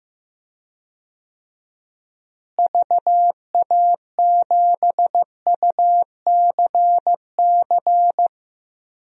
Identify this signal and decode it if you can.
Morse code, heard as a keyed tone, about 15 wpm, 700 Hz. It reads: VA7UCC
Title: DXCC-100 Call Signs - 6 Characters - Repeated 2x - 15wpm